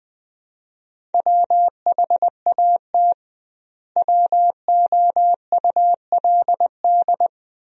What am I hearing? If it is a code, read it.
Morse code, heard as a keyed tone, about 20 wpm, 700 Hz. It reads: WHAT WOULD
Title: 2x Word-Spacing - Sentences from Top 200 Words - 20wpm